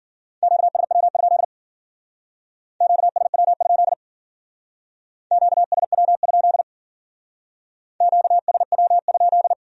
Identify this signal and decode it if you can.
Morse code, heard as a keyed tone, about 49 wpm, 700 Hz. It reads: QSW? QSW? QSW? QSW?
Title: Speed-Racing Q-Codes - 40wpm